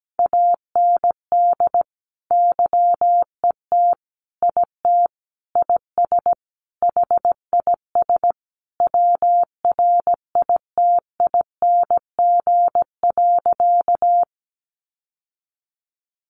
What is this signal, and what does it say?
Morse code, heard as a keyed tone, about 17 wpm, 700 Hz. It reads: AND YET IT IS HIS WRITING.
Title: Sentences from Top 800 Words 17wpm